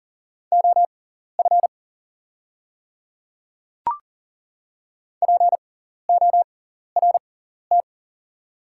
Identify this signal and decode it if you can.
Morse code, heard as a keyed tone, about 40 wpm, spaced as for 14 wpm, 700 Hz. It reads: OF PORT
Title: 6x Character Spacing - Top 200 Words - 40wpm